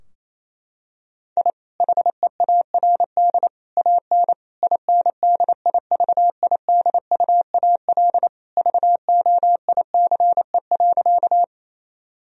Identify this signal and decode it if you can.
Morse code, heard as a keyed tone, about 28 wpm, 700 Hz. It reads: I HEARD AN INDIVIDUAL VOICE.
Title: Sentences from Top 1000 Words - Review Part 4 - 28wpm